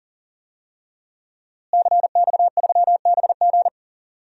Morse code, heard as a keyed tone, about 40 wpm, 700 Hz. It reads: CX3BG
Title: DXCC-100 Call Signs - 5 Characters - Repeated 2x - 40wpm